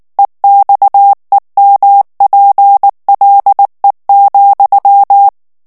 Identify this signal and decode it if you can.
Morse code, heard as a keyed tone, about 19 wpm, 800 Hz. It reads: EXEMPLE,
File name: morse mp3 / FRANCAIS TEXTE CLAIR 1 VITESSE 18 MOTS MINUTES